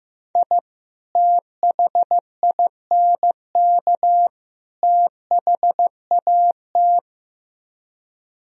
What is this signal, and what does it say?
Morse code, heard as a keyed tone, about 15 wpm, 700 Hz. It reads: I THINK THAT
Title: Speed-Racing - Sets of 3 Words 15wpm